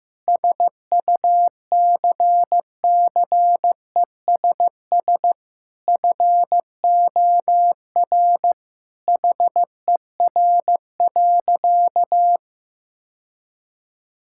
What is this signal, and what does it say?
Morse code, heard as a keyed tone, about 16 wpm, 700 Hz. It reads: SUCCESS FOR HER.
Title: Sentences from Top 1000 Words - Review Part 6 - 15wpm